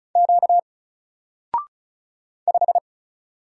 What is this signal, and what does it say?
Morse code, heard as a keyed tone, about 36 wpm, 700 Hz. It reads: Q 5
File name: Single-Letter-Number-35wpm